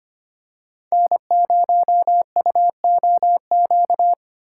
Morse code, heard as a keyed tone, about 25 wpm, 700 Hz. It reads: N0UOQ
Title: USA Call Signs - 5 Characters - 6x Character Spacing - 25wpm